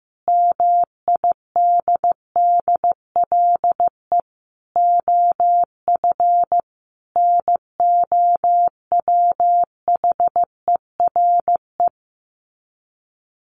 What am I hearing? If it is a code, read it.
Morse code, heard as a keyed tone, about 15 wpm, 700 Hz. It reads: MIDDLE OF NOWHERE
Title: Sets of 5 Words Encore 15wpm